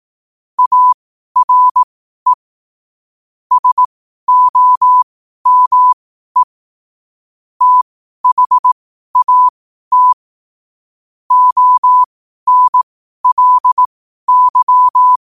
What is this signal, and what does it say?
Morse code, heard as a keyed tone, about 18 wpm, spaced as for 12 wpm, 1000 Hz. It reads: ARE SOME THAT ONLY
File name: qotdfiles / 12wpm